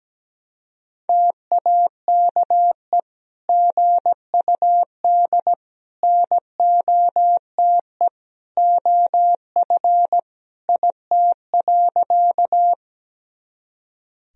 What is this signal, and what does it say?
Morse code, heard as a keyed tone, about 17 wpm, 700 Hz. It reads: TAKE GUD NOTE OF IT.
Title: Sentences from Top 500 Words and Abbreviations 17wpm